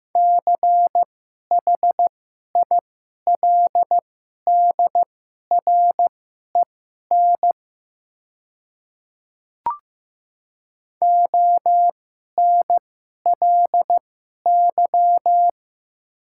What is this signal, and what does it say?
Morse code, heard as a keyed tone, about 15 wpm, spaced as for 11 wpm, 700 Hz. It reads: CHILDREN ONLY
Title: 2x Character Spacing - Top 300 Words - 15wpm